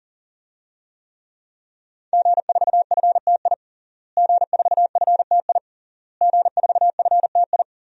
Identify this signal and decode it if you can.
Morse code, heard as a keyed tone, about 40 wpm, 700 Hz. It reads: G4FTI G4FTI G4FTI
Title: DXCC-100 Call Signs - 5 Characters - Repeated 3x - 40wpm